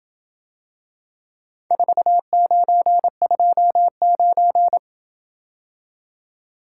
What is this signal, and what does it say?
Morse code, heard as a keyed tone, about 28 wpm, 700 Hz. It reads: 4929